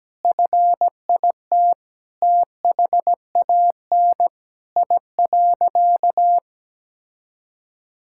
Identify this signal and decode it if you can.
Morse code, heard as a keyed tone, about 17 wpm, 700 Hz. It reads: FIT THAN I.
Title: Sentences from Top 1000 Words - Review Part 5 - 17wpm